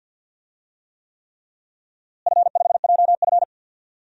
Morse code, heard as a keyed tone, about 50 wpm, 700 Hz. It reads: F5JF
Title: DXCC-100 Call Signs - 4 Characters - Repeated 3x - 50wpm